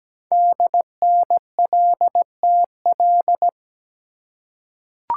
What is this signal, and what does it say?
Morse code, heard as a keyed tone, about 17 wpm, 700 Hz. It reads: DNLTL
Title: Code Groups - 5 Letters - 17wpm